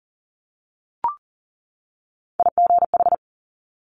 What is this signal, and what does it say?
Morse code, heard as a keyed tone, about 39 wpm, 700 Hz. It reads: IGH
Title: Multi-letter Phonemes 40wpm